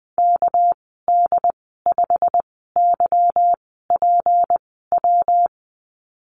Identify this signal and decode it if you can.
Morse code, heard as a keyed tone, about 20 wpm, spaced as for 14 wpm, 700 Hz. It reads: KD5YPW
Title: 245 Lesson 39g - Letter z - Call Signs 20wpm